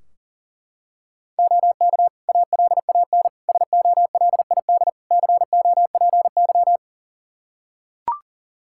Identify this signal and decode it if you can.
Morse code, heard as a keyed tone, about 37 wpm, 700 Hz. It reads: OK ALAN SOLID COPY
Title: Practice QSO - Exchange 3 - More Info - 40wpm